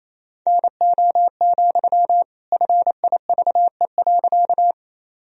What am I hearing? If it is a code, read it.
Morse code, heard as a keyed tone, about 28 wpm, 700 Hz. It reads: NO, FIVE.
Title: Sentences from Top 1000 Words - Review Part 2 - 28wpm